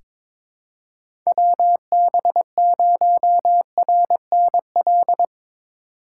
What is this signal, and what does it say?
Morse code, heard as a keyed tone, about 22 wpm, 700 Hz. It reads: WB0RNL
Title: USA Call Signs - 6 Characters - 6x Character Spacing - 22wpm